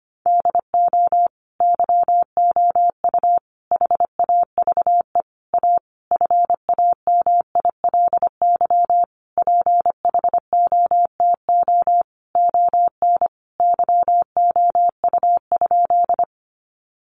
Morse code, heard as a keyed tone, about 25 wpm, 700 Hz. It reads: DO YOU HAVE A FAMILY PHOTO ON YOU?
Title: Sentences from Top 1200 Words 25wpm